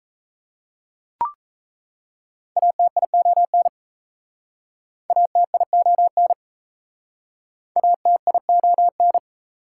Morse code, heard as a keyed tone, about 37 wpm, 690 Hz. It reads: ATION ATION ATION